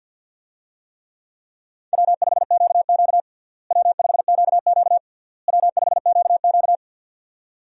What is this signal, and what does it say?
Morse code, heard as a keyed tone, about 48 wpm, 700 Hz. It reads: W5QQ W5QQ W5QQ
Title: USA Call Signs - 4 Characters - Repeated 3x - 50wpm